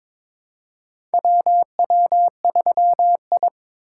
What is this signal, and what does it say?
Morse code, heard as a keyed tone, about 22 wpm, 700 Hz. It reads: WW3I
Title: USA Call Signs - 3x Character Spacing - 22wpm